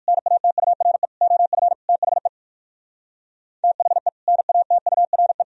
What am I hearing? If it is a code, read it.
Morse code, heard as a keyed tone, about 58 wpm, 700 Hz. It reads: NATURE OF THE THE NATURE